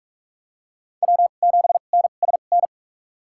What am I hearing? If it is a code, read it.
Morse code, heard as a keyed tone, about 44 wpm, spaced as for 32 wpm, 690 Hz. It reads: W7NSN
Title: USA Call Signs - 5 Characters - 2x Character Spacing - 45wpm